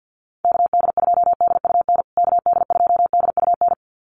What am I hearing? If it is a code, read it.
Morse code, heard as a keyed tone, about 50 wpm, 700 Hz. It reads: KD2DUN KD2DUN
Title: USA Call Signs Repeated 2x 50wpm